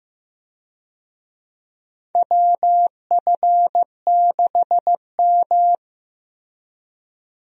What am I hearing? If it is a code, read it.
Morse code, heard as a keyed tone, about 15 wpm, 700 Hz. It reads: WF6M
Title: USA Call Signs - 4 Characters - Repeated 3x - 15wpm